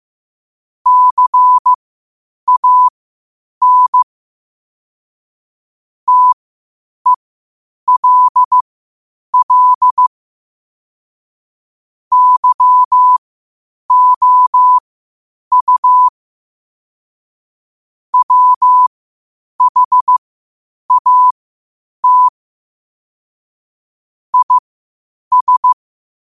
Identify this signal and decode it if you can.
Morse code, heard as a keyed tone, about 15 wpm, spaced as for 8 wpm, 1000 Hz. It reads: CAN TELL YOU WHAT IS
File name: qotdfiles / 8wpm